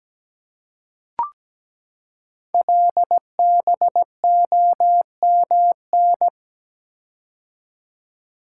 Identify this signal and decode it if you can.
Morse code, heard as a keyed tone, about 17 wpm, 700 Hz. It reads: LBOMN
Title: Code Groups - 5 Letters - 17wpm